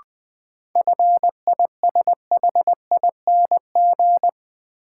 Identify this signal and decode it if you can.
Morse code, heard as a keyed tone, about 21 wpm, 700 Hz. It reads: FISHING
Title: Sentences from Top 1000 Words - Review Part 5 - New Word Drill - 20wpm